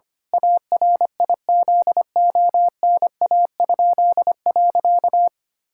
Morse code, heard as a keyed tone, about 25 wpm, 700 Hz. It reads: ARIZONA?.
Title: Pangram Challenge v2 - 25wpm